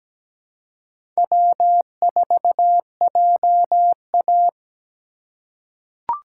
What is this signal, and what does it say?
Morse code, heard as a keyed tone, about 17 wpm, 700 Hz. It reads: W4JA
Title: USA Call Signs - 4 Characters - 4x Character Spacing - 17wpm